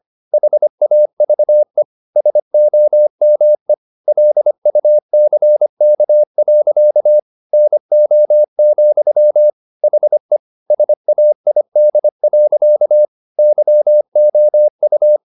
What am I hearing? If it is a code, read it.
Morse code, heard as a keyed tone, about 25 wpm, 600 Hz. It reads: HAVE SOME LUCK. NO, HE SAID. YOU